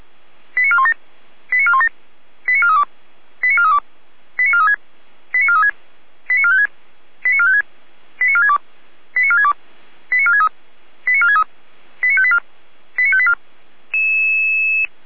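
Five-tone selective call (ZVEI), heard as quick runs of five tones, tones of about 70 ms. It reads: 89417 89417 89432 89432 89436 89436 89456 89456 89462 89462 89463 89463 89584 89584
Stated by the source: ZVEI five-tone selective call received by SDR.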